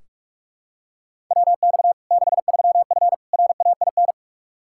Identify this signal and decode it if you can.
Morse code, heard as a keyed tone, about 46 wpm, 700 Hz. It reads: WX 63F RAIN
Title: Practice QSO - Exchange 2 - Info - 45wpm